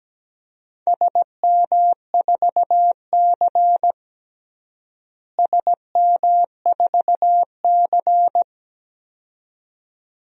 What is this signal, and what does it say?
Morse code, heard as a keyed tone, about 17 wpm, 700 Hz. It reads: SM4C SM4C
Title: DXCC-100 Call Signs - 4 Characters - Repeated 2x - 17wpm